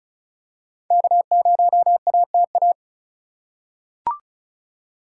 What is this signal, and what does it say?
Morse code, heard as a keyed tone, about 36 wpm, 700 Hz. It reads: K0ATA
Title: USA Call Signs - 8x Character Spacing - 35wpm